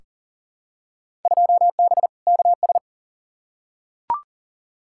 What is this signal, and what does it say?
Morse code, heard as a keyed tone, about 40 wpm, 700 Hz. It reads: 2B KS
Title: Field Day Exchanges Encore - 40wpm